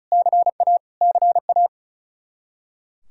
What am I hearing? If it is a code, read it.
Morse code, heard as a keyed tone, about 36 wpm, 700 Hz. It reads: CA CA
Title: Practice QSO - Exchange 1 - Beginning - 35wpm